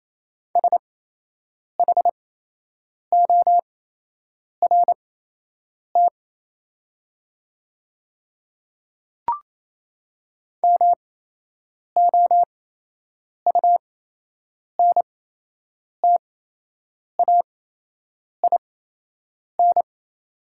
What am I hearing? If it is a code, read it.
Morse code, heard as a keyed tone, about 28 wpm, spaced as for 8 wpm, 700 Hz. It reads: SHORT MOUNTAIN
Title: 8x Character Spacing - Top 500 Words - 28wpm